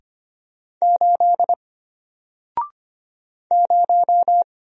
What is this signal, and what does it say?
Morse code, heard as a keyed tone, about 25 wpm, 700 Hz. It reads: 8 0
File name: Single-Letter-Number-25wpm